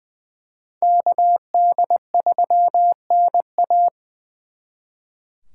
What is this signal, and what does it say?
Morse code, heard as a keyed tone, about 23 wpm, 700 Hz. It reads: KD3NA
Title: Speed Racing - USA 2x2 Call Signs 20wpm